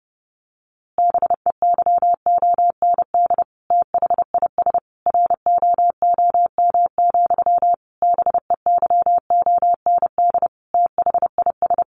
Morse code, heard as a keyed tone, about 30 wpm, 700 Hz. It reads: BEYOND THIS ROOM, BEYOND THIS
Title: A Sound Of Thunder 30wpm